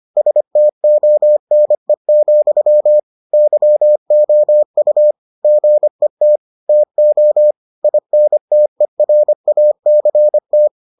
Morse code, heard as a keyed tone, about 25 wpm, 600 Hz. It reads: STONE, YOU GET TO INTERACT